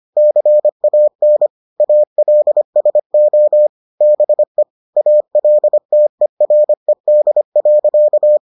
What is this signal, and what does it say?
Morse code, heard as a keyed tone, about 25 wpm, 600 Hz. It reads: CAN ALSO BE ALTERED.